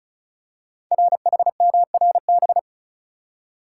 Morse code, heard as a keyed tone, about 36 wpm, 700 Hz. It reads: RHMRB
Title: Code Groups - 5 Letters - 35wpm